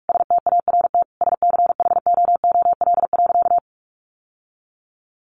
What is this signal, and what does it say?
Morse code, heard as a keyed tone, about 44 wpm, 700 Hz. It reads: START SCHOOL.
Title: Sentences from Top 1100 Words 45wpm